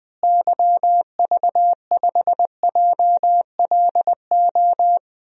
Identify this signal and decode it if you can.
Morse code, heard as a keyed tone, about 21 wpm, 700 Hz. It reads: YV5JLO
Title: DXCC-100 Call Signs - 6 Characters - Repeated 3x - 20wpm